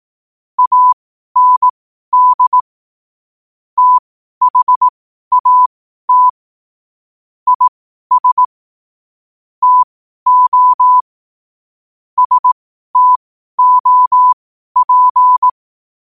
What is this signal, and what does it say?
Morse code, heard as a keyed tone, about 18 wpm, spaced as for 12 wpm, 1000 Hz. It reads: AND THAT IS TO STOP
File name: qotdfiles / 12wpm